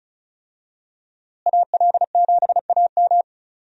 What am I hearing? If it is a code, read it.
Morse code, heard as a keyed tone, about 35 wpm, 700 Hz. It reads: AL7AM
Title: USA Call Signs - 5 Characters - Repeated 2x - 35wpm